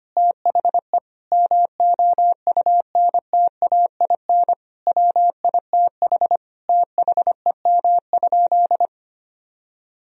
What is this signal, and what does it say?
Morse code, heard as a keyed tone, about 25 wpm, 700 Hz. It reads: THE MOUNTAIN WITH THEM?
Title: Sentences from Top 1000 Words - Review Part 3 - 25wpm